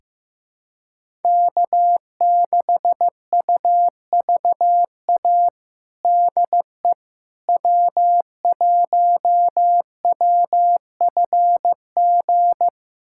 Morse code, heard as a keyed tone, about 15 wpm, 700 Hz. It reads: K6UVA DE W1WFG
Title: USA Callsign DE Callsign - 5 Characters - 15wpm